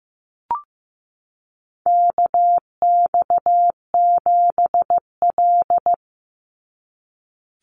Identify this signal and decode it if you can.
Morse code, heard as a keyed tone, about 15 wpm, 700 Hz. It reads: KX7L
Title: NA SOTA Chaser Callsigns 15wpm